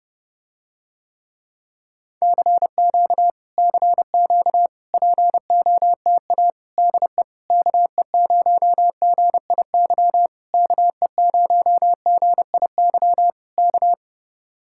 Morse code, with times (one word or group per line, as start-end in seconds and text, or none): 2.2-3.3 CQ
3.6-4.7 CQ
4.9-6.5 POTA
6.8-7.2 DE
7.5-10.3 KE0GIY
10.5-13.3 KE0GIY
13.6-13.9 K